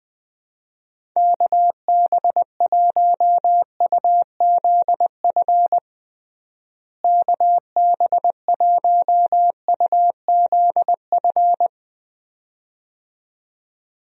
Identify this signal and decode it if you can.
Morse code, heard as a keyed tone, about 21 wpm, 700 Hz. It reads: KB1UZF KB1UZF
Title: USA Call Signs - 6 Characters - Repeated 2x - 20wpm